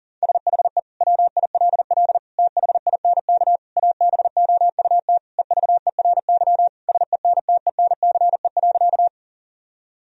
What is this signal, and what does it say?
Morse code, heard as a keyed tone, about 40 wpm, 700 Hz. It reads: SHE WILL THINK ABOUT EVERY SENTENCE.
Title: Sentences from Top 1000 Words - Review Part 1 - 40wpm